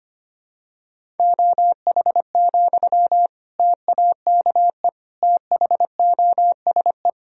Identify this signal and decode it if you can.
Morse code, heard as a keyed tone, about 25 wpm, 700 Hz. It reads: OH, TAKE THOSE
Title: Sentences from Top 1000 Words - Review Part 1 - 25wpm